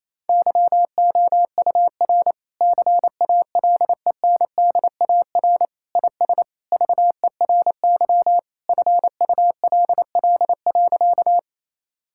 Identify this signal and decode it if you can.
Morse code, heard as a keyed tone, about 28 wpm, 700 Hz. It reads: YOUR CALENDAR IS VERY FULL.
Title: Sentences from Top 1600 Words 28wpm